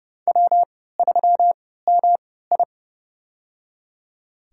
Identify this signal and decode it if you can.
Morse code, heard as a keyed tone, about 30 wpm, spaced as for 17 wpm, 700 Hz. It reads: W3MI
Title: USA Call Signs - 3x Character Spacing - 30wpm